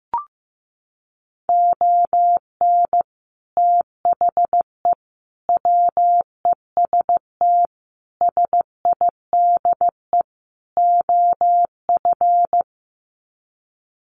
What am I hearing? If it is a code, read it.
Morse code, heard as a keyed tone, about 15 wpm, 700 Hz. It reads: ON THE WEST SIDE OF
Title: Sets of 5 Words Encore2 15wpm